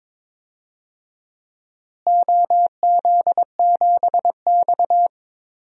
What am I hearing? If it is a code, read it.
Morse code, heard as a keyed tone, about 22 wpm, 700 Hz. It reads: OZ7X
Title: DXCC-100 Call Signs - 4 Characters - Repeated 3x - 22wpm